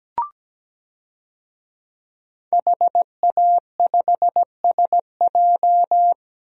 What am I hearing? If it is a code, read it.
Morse code, heard as a keyed tone, about 17 wpm, 700 Hz. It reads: HA5SJ
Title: DXCC-100 Call Signs - 5 Characters - Repeated 3x - 17wpm